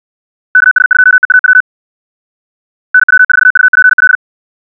Morse code, heard as a keyed tone, about 64 wpm, 1500 Hz. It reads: KS2EAW NW4UPL
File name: ja7uhv / SampleHST3